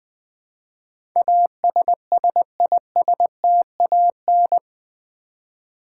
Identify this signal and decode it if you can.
Morse code, heard as a keyed tone, about 20 wpm, 700 Hz. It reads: ASSISTAN
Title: Build A Word - 4 Letters At A Time - 10-Letter Words - 20wpm